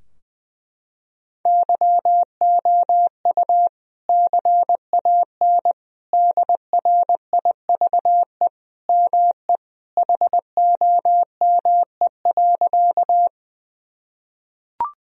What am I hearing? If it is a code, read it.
Morse code, heard as a keyed tone, about 21 wpm, 700 Hz. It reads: YOU CAN DRIVE ME HOME.
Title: Sentences from Top 1000 Words - Review Part 2 - 20wpm